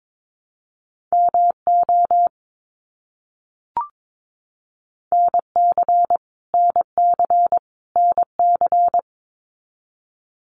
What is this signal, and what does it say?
Morse code, heard as a keyed tone, about 22 wpm, 700 Hz. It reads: MO NC NC NC
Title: US States Repeated 3x 22wpm